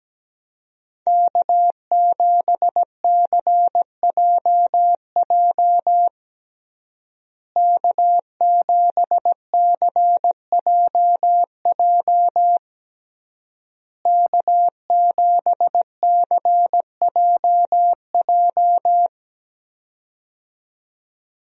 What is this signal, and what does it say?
Morse code, heard as a keyed tone, about 17 wpm, 700 Hz. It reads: K7CJJ K7CJJ K7CJJ
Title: USA Call Signs - 5 Characters - Repeated 3x - 17wpm